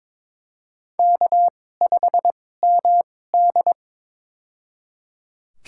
Morse code, heard as a keyed tone, about 22 wpm, spaced as for 16 wpm, 700 Hz. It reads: K5MD
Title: USA Call Signs - 4 Characters - 2x Character Spacing - 22wpm